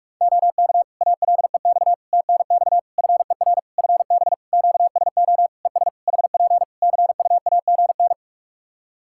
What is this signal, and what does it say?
Morse code, heard as a keyed tone, about 46 wpm, 700 Hz. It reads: OK ALEX TNX FER FB QSO ES HP CUAGN
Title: Practice QSO - Exchange 4 - Ending - 45wpm